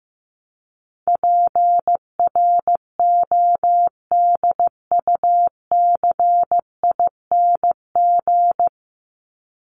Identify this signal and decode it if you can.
Morse code, heard as a keyed tone, about 15 wpm, 700 Hz. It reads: PRODUCING
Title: ING Suffix - Top 100 Words with Base Words 15wpm